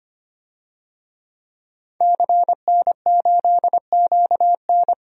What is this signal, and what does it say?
Morse code, heard as a keyed tone, about 25 wpm, 700 Hz. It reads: CN8QN
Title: DXCC-100 Call Signs - 5 Characters - 25wpm